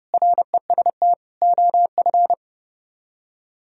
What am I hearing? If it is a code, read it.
Morse code, heard as a keyed tone, about 35 wpm, 700 Hz. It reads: REST OF
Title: Speed-Racing - Sets of 4 Words 30wpm